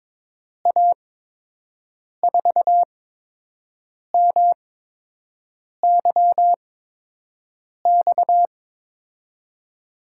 Morse code, heard as a keyed tone, about 22 wpm, spaced as for 6 wpm, 700 Hz. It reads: A4MYX
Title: USA Call Signs - 6 Characters - 8x Character Spacing - 22wpm